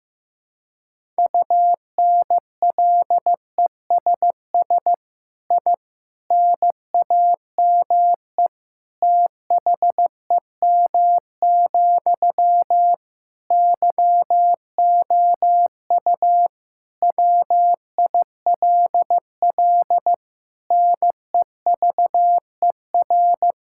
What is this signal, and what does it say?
Morse code, heard as a keyed tone, about 15 wpm, 700 Hz. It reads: UNLESS I NAME THEM, YOU WILL NEVER
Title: Sentences from Top 1000 Words - Review Part 6 - 15wpm